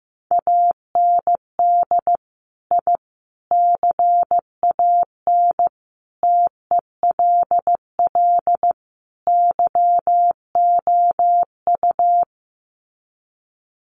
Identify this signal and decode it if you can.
Morse code, heard as a keyed tone, about 15 wpm, 700 Hz. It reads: AND I CAN TELL YOU
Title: Sets of 5 Words Encore2 15wpm